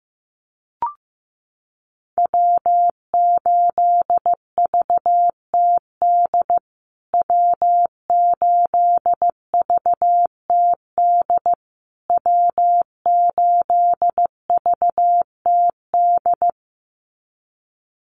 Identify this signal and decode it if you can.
Morse code, heard as a keyed tone, about 15 wpm, 700 Hz. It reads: W8VTD W8VTD W8VTD
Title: USA Call Signs Repeated 3x 15wpm